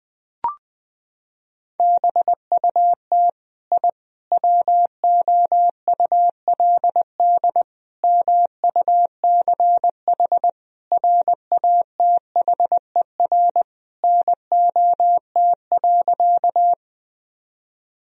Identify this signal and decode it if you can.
Morse code, heard as a keyed tone, about 20 wpm, 700 Hz. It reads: BUT I WOULD MUCH RATHER NOT.
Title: Sentences from Top 1000 Words - Review Part 4 - 20wpm